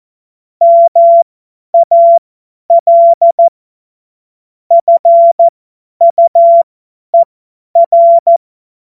Morse code, heard as a keyed tone, about 14 wpm, spaced as for 10 wpm, 680 Hz. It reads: MAL FUER
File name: Morsekurs / Witze / Witz 27. Juli 2021, 50 F70